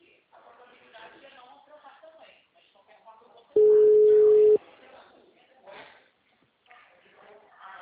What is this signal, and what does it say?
Ringback tone (Europe, 425 Hz), with speech.
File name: out- -2014- - - .18020